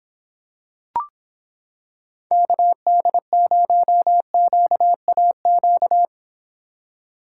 Morse code, heard as a keyed tone, about 26 wpm, 700 Hz. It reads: KD0QAQ